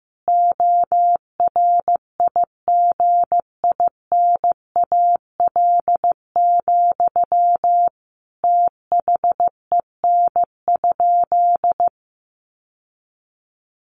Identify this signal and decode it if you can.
Morse code, heard as a keyed tone, about 15 wpm, 700 Hz. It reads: ORIGINAL, THEN?
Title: Sentences from Top 900 Words 15wpm